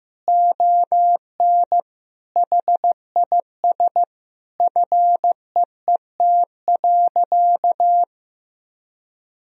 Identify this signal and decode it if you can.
Morse code, heard as a keyed tone, about 15 wpm, 700 Hz. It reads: ON HIS FEET.
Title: Sentences from Top 1000 Words - Review Part 1 - 15wpm